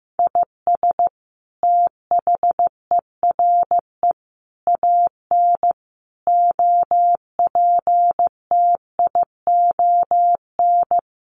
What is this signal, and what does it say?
Morse code, heard as a keyed tone, about 15 wpm, 700 Hz. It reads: IS THERE AN OPTION
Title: Sentences from Top 1300 Words 15wpm